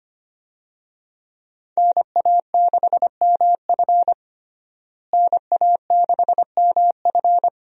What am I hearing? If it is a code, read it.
Morse code, heard as a keyed tone, about 25 wpm, 700 Hz. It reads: NA6MF NA6MF
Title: USA Call Signs - 5 Characters - Repeated 2x - 25wpm